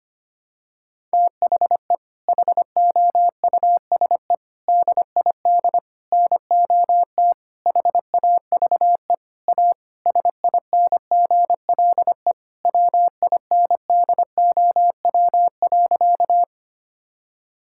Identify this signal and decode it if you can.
Morse code, heard as a keyed tone, about 25 wpm, 700 Hz. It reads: THE HOUSE DID NOT HAVE A SINGLE WINDOW.
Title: Sentences from Top 1600 Words 25wpm